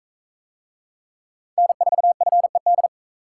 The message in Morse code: N4FED

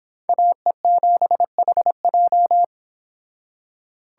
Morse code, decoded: AE7HJ

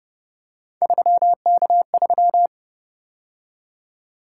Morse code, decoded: 3K3

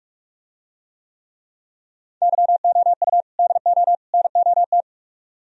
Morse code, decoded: YOU DO NOT

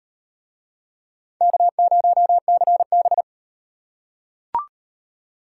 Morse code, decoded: K0CB

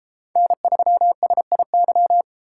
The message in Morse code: N3SIY

